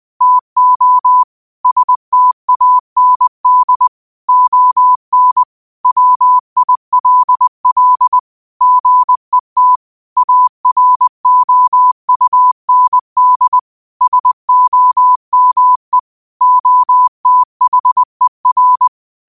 Morse code, heard as a keyed tone, about 20 wpm, 1000 Hz. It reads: TO STAND ON WILL GET AROUND SOME OTHER
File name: 20wpm